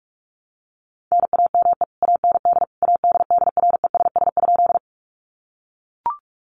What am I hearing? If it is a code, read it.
Morse code, heard as a keyed tone, about 44 wpm, 700 Hz. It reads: NAME AND ADDRESS?
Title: Sentences from Top 1000 Words 45wpm